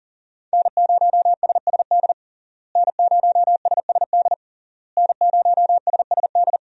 Morse code, heard as a keyed tone, about 40 wpm, 700 Hz. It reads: N0SSD N0SSD N0SSD
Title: USA Call Signs - 5 Characters - Repeated 3x - 40wpm